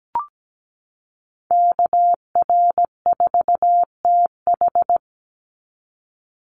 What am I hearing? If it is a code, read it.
Morse code, heard as a keyed tone, about 17 wpm, 700 Hz. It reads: KR4TH
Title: Call Signs - 17wpm